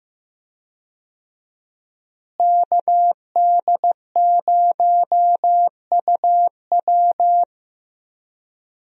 Morse code, heard as a keyed tone, about 15 wpm, 700 Hz. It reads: KD0UW
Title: USA Call Signs - 5 Characters - Repeated 2x - 15wpm